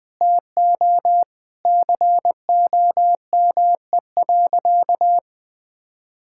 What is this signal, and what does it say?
Morse code, heard as a keyed tone, about 20 wpm, 700 Hz. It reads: TO COME.
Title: Sentences with 5-Word Phrases - 20wpm